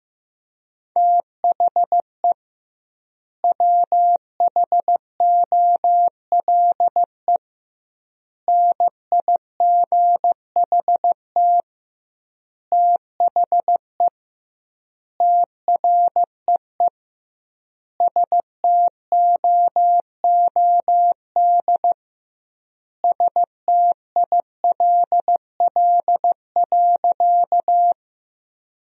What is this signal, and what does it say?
Morse code, heard as a keyed tone, about 16 wpm, 700 Hz. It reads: THE WHOLE NIGHT THE TREE STOOD STILL.
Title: 2x Word-Spacing - Sentences from Top 500 Words - 15wpm